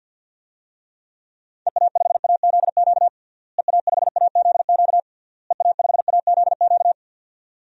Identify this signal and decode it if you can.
Morse code, heard as a keyed tone, about 50 wpm, 700 Hz. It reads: EA5AZQ EA5AZQ EA5AZQ
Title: DXCC-100 Call Signs - 6 Characters - Repeated 3x - 50wpm